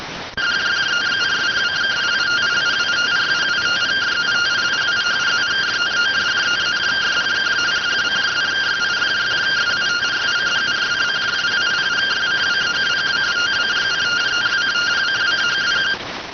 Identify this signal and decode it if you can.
Two-tone teleprinter signal, about 170 Hz shift.